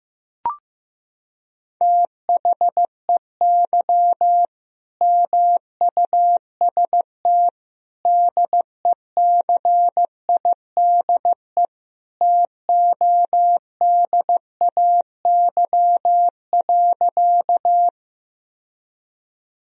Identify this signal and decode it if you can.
Morse code, heard as a keyed tone, about 15 wpm, 700 Hz. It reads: THEY MUST DECIDE TODAY.
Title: Sentences from Top 1000 Words - Review Part 3 - 15wpm